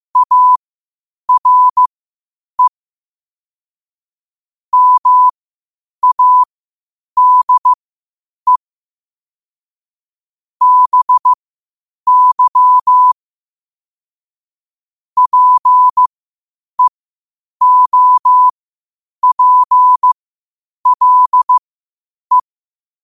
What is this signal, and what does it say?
Morse code, heard as a keyed tone, about 15 wpm, spaced as for 8 wpm, 1000 Hz. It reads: ARE MADE BY PEOPLE